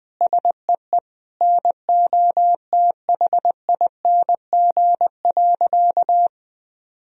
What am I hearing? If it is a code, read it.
Morse code, heard as a keyed tone, about 20 wpm, 700 Hz. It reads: SEE NOTHING.
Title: Sentences from Top 1000 Words - Review Part 2 - 20wpm